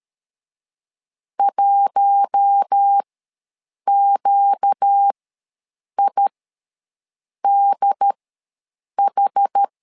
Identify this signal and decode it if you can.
Morse code, heard as a keyed tone, about 13 wpm, spaced as for 7 wpm, 800 Hz. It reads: 1QIDH
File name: LEZIONI / 4 Q1 ws 12wpm 2129